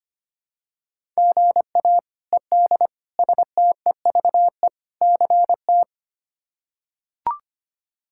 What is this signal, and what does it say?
Morse code, heard as a keyed tone, about 25 wpm, 700 Hz. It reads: GA ED STEVE CT
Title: K1USN SST QSO Practice - 25wpm